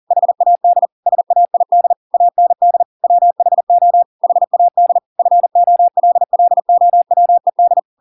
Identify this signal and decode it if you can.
Morse code, heard as a keyed tone, about 40 wpm, 700 Hz. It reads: HAD SAID AND WHO HAD FOLLOWED